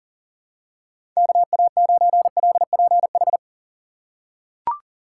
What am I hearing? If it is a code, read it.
Morse code, heard as a keyed tone, about 40 wpm, 700 Hz. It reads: KA9LPH
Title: USA Call Signs - 6 Characters - 6x Character Spacing - 40wpm